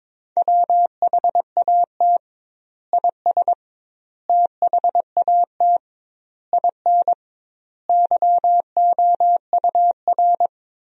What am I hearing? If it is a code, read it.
Morse code, heard as a keyed tone, about 22 wpm, 700 Hz. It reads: WHAT IS THAT IN YOUR
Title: 2x Word-Spacing - Sentences from Top 200 Words - 22wpm